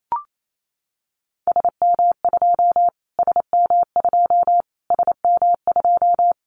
Morse code, heard as a keyed tone, about 28 wpm, 700 Hz. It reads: SM2 SM2 SM2
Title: Callsign Prefixes for Top 25 DXCC - Repeated 3x - 28wpm